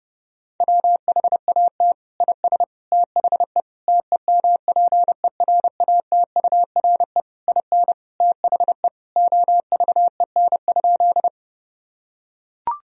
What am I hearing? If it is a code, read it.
Morse code, heard as a keyed tone, about 30 wpm, 700 Hz. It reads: WHAT IS THE TEMPERATURE IN THE OVEN?
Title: Sentences from Top 1000 Words - Review Part 4 - 30wpm